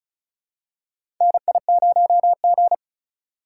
Morse code, heard as a keyed tone, about 36 wpm, 700 Hz. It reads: NI0G
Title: USA Call Signs - 4 Characters - 3x Character Spacing - 35wpm